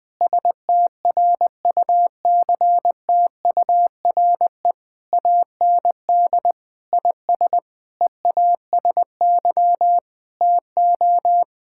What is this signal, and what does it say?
Morse code, heard as a keyed tone, about 20 wpm, 700 Hz. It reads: STRUCTURE AND IS EASY TO
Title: Sentences from Top 1000 Words - Review Part 6 - 20wpm